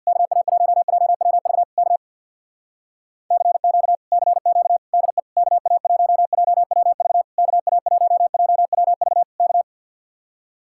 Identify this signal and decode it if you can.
Morse code, heard as a keyed tone, about 62 wpm, 700 Hz. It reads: KA1JWV K CQ CQ DE KA1JWV KA1JWV K
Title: Speed-Racing Common QSO Phrases - 50wpm